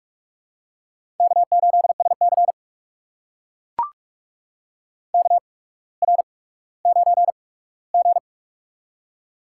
Morse code, heard as a keyed tone, about 44 wpm, spaced as for 19 wpm, 700 Hz. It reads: K8SC KR9G
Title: USA Call Signs - 4 Characters - 8x Character Spacing - 45wpm